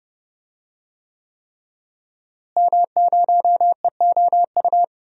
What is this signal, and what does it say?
Morse code, heard as a keyed tone, about 30 wpm, 700 Hz. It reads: M0EOU
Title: DXCC-100 Call Signs - 5 Characters - Repeated 3x - 30wpm